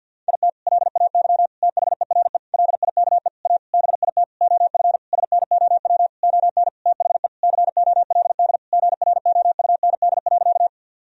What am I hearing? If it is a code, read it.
Morse code, heard as a keyed tone, about 47 wpm, 700 Hz. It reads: IT LAY THERE LIKE A BIT OF SNOW ON THE COLD GROUND.
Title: Sentences from Top 1000 Words - Review Part 3 - 50wpm